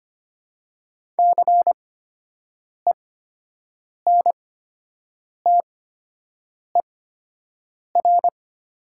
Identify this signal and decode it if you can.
Morse code, heard as a keyed tone, about 25 wpm, spaced as for 7 wpm, 700 Hz. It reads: CENTER
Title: 8x Character Spacing - Top 500 Words - 25wpm